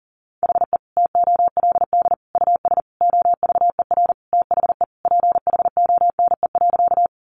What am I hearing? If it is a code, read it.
Morse code, heard as a keyed tone, about 40 wpm, 700 Hz. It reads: HE TOLD US OVER THE PHONE.
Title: Sentences from Top 1100 Words 40wpm